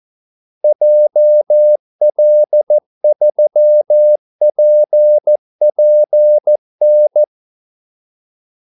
Morse code, heard as a keyed tone, about 14 wpm, 600 Hz. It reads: JL3PPN
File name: Callsigns-14wpm